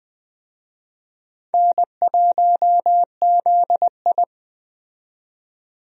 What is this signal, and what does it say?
Morse code, heard as a keyed tone, about 20 wpm, 700 Hz. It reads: N1ZI
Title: USA Call Signs - 4 Characters - 20wpm